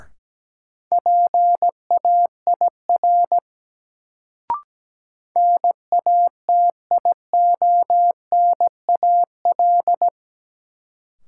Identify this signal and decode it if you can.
Morse code, heard as a keyed tone, about 17 wpm, 700 Hz. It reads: PAIR NATIONAL
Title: Sentences from Top 1200 Words - New Word Drill Alt - 17wpm